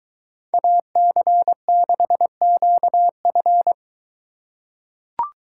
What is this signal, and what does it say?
Morse code, heard as a keyed tone, about 21 wpm, 700 Hz. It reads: AC6QF